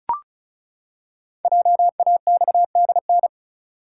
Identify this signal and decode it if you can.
Morse code, heard as a keyed tone, about 36 wpm, 700 Hz. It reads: JAXDN
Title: Code Groups - 5 Letters - 35wpm